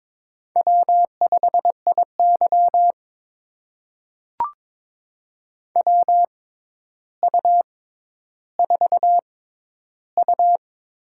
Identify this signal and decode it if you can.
Morse code, heard as a keyed tone, about 23 wpm, spaced as for 11 wpm, 700 Hz. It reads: W5IY WU4U
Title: USA Call Signs - 4 Characters - 6x Character Spacing - 22wpm